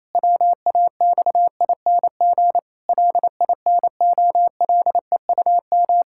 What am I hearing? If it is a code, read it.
Morse code, heard as a keyed tone, about 28 wpm, 700 Hz. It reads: WAXING LINOLEUM
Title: Pangram Challenge v2 - 28wpm